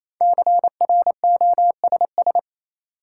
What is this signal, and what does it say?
Morse code, heard as a keyed tone, about 28 wpm, 700 Hz. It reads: CROSS
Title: Sentences from Top 300 Words - New Word Drill - 28wpm